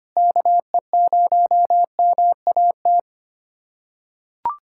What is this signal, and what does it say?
Morse code, heard as a keyed tone, about 25 wpm, 700 Hz. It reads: KE0MAT
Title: USA Callsign DE Callsign - 6 Characters - 25wpm